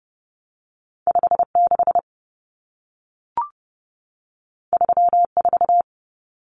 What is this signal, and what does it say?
Morse code, heard as a keyed tone, about 30 wpm, 700 Hz. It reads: 56 34
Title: Numbers - 2-Digit 30wpm